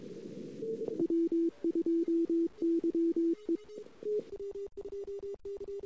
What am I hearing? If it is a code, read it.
Morse code, heard as a keyed tone, about 22 wpm, 330 Hz. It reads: W2YE
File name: CW / 2008-ss-cw-k5tr-op-n5rz -23:00